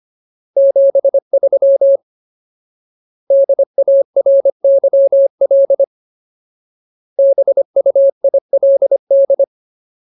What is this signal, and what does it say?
Morse code, heard as a keyed tone, about 25 wpm, 550 Hz. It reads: 73 DARYL BUILD
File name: Ultimate Word List / 25WPM Ultimate